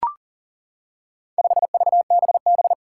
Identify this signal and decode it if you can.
Morse code, heard as a keyed tone, about 40 wpm, 700 Hz. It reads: 5VBB
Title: Four Challenging Letter-Number Combinations 40wpm